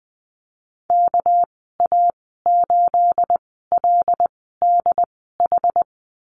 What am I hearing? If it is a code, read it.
Morse code, heard as a keyed tone, about 20 wpm, spaced as for 14 wpm, 700 Hz. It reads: KA8LDH
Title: 231 Lesson 37g - Letter x - Call Signs 20wpm